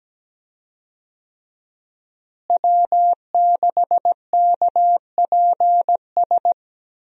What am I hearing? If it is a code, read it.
Morse code, heard as a keyed tone, about 17 wpm, 700 Hz. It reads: W6KPS
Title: USA Call Signs - 5 Characters - Repeated 2x - 17wpm